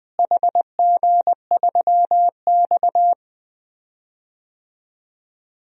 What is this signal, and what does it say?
Morse code, heard as a keyed tone, about 20 wpm, 700 Hz. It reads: HG3X
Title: DXCC-100 Call Signs - 4 Characters - Repeated 2x - 20wpm